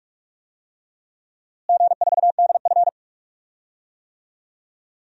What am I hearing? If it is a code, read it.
Morse code, heard as a keyed tone, about 46 wpm, 700 Hz. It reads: G4DF
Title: DXCC-100 Call Signs - 4 Characters - Repeated 2x - 45wpm